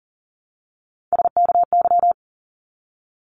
Morse code, heard as a keyed tone, about 39 wpm, 700 Hz. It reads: SKY
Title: Multi-letter Phonemes 40wpm